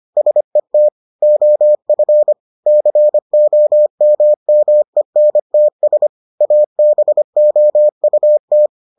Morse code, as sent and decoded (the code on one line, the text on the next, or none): ... - / --- ..-. / -.-. --- -- -- . -. - ... / .- -... --- ..- -
SET OF COMMENTS ABOUT